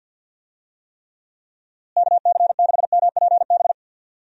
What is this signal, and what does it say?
Morse code, heard as a keyed tone, about 50 wpm, 700 Hz. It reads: KC6MPB
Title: USA Call Signs - 6 Characters - Repeated 3x - 50wpm